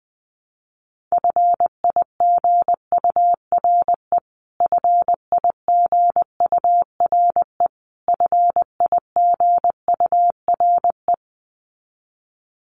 Morse code, heard as a keyed tone, about 20 wpm, 700 Hz. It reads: FIGURE FIGURE FIGURE
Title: Top 500 Words Repeated 20wpm